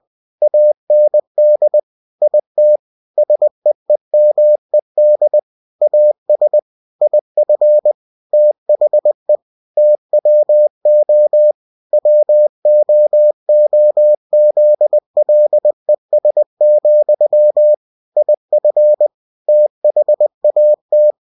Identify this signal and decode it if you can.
Morse code, heard as a keyed tone, about 20 wpm, 600 Hz. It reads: AND IT SEEMED AS IF THE TWO WOOZLES, IF THAT
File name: cw-books / winnie-the-pooh / winnie-20wpm-page-0022